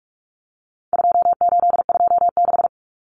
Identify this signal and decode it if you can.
Morse code, heard as a keyed tone, about 44 wpm, 700 Hz. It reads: 2826
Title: Numbers - 4-Digit 45wpm